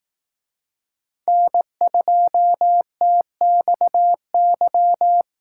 Morse code, heard as a keyed tone, about 18 wpm, 700 Hz. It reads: N2TXY
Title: Speed Racing - USA 1x3 Call Signs 15wpm